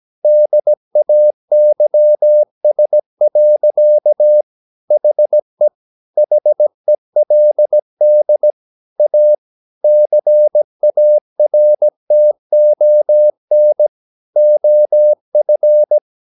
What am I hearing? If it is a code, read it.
Morse code, heard as a keyed tone, about 17 wpm, 600 Hz. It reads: DAYS. HE HELD A CARTON OF
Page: Romanen Never af Ken Follet på engelsk.